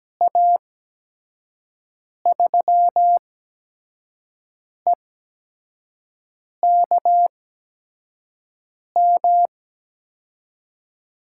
Morse code, heard as a keyed tone, about 17 wpm, spaced as for 5 wpm, 700 Hz. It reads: A3EKM
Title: USA Call Signs - 8x Character Spacing - 17wpm